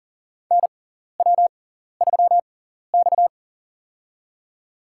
Morse code, heard as a keyed tone, about 44 wpm, spaced as for 14 wpm, 700 Hz. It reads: NW3X
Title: USA Call Signs - 4 Characters - 6x Character Spacing - 40wpm